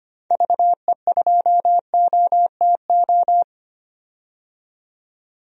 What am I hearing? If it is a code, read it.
Morse code, heard as a keyed tone, about 25 wpm, 700 Hz. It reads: VE2OTO
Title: DXCC-100 Call Signs - 6 Characters - 25wpm